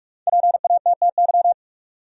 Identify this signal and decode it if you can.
Morse code, heard as a keyed tone, about 46 wpm, 700 Hz. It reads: PATTY
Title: USA Common Names - 45wpm